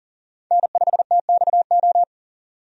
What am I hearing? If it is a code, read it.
Morse code, heard as a keyed tone, about 40 wpm, 700 Hz. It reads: N5TXO
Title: USA Call Signs - 5 Characters - 40wpm